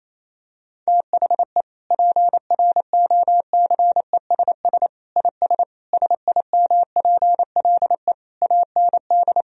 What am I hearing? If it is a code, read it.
Morse code, heard as a keyed tone, about 28 wpm, 700 Hz. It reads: THE PROCESS IS SIMPLE AND
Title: Sentences from Top 1000 Words - Review Part 3 - 28wpm